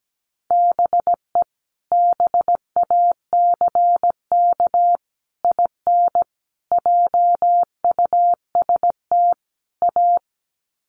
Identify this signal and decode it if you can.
Morse code, heard as a keyed tone, about 17 wpm, 700 Hz. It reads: BE BACK IN JUST A
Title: Sets of 5 Words Encore 17wpm